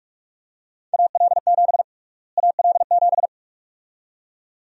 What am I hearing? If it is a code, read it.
Morse code, heard as a keyed tone, about 46 wpm, 700 Hz. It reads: AL7 AL7
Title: USA Call Sign Prefixes - Repeated 2x - 45wpm